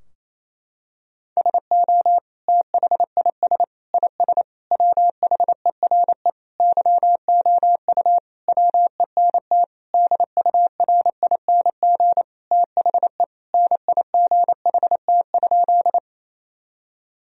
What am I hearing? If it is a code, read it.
Morse code, heard as a keyed tone, about 28 wpm, 700 Hz. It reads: SO THIS IS WHERE YOU WENT DURING THE NIGHT?
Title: Sentences from Top 1000 Words - Review Part 2 - 28wpm